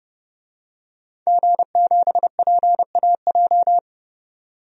Morse code, heard as a keyed tone, about 31 wpm, 700 Hz. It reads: G7PAJ
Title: DXCC-100 Call Signs - 5 Characters - 30wpm